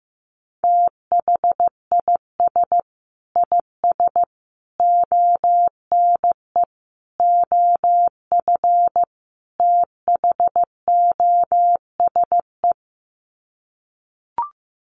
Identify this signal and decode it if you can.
Morse code, heard as a keyed tone, about 15 wpm, 700 Hz. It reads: THIS IS ONE OF THOSE
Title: Sets of 5 Words Encore2 15wpm